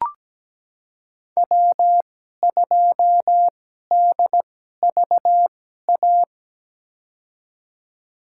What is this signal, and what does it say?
Morse code, heard as a keyed tone, about 17 wpm, spaced as for 12 wpm, 700 Hz. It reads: W2DVA
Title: USA Call Signs - 5 Characters - 2x Character Spacing - 17wpm